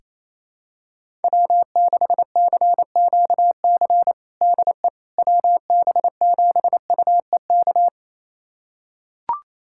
Morse code, heard as a keyed tone, about 28 wpm, 700 Hz. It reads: W6CQC DE WB7UEK
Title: USA Callsign DE Callsign 28wpm